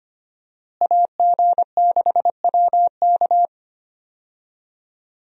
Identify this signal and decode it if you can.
Morse code, heard as a keyed tone, about 25 wpm, 700 Hz. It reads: AG6WK
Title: USA Call Signs - 5 Characters - Repeated 3x - 25wpm